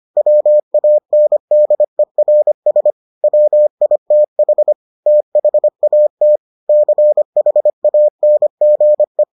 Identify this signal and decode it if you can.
Morse code, heard as a keyed tone, about 25 wpm, 600 Hz. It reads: WANDERS WITH THAT CHANGE